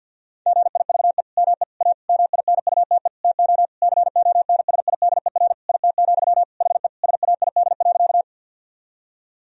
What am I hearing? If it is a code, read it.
Morse code, heard as a keyed tone, about 50 wpm, 700 Hz. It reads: GIVE ME A MINUTE TO CONSIDER IT, HE SAID.
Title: Sentences from Top 1000 Words - Review Part 4 - 50wpm